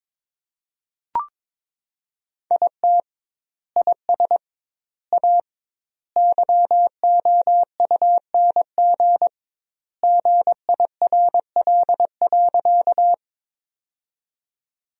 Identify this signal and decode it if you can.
Morse code, heard as a keyed tone, about 22 wpm, 700 Hz. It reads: IT IS A YOUNG GIRL.
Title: 2x Word-Spacing - Sentences from Top 400 Words - 22wpm